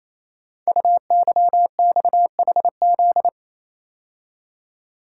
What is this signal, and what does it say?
Morse code, heard as a keyed tone, about 28 wpm, 700 Hz. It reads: UYXHZ
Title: Code Groups - 5 Letters - 28wpm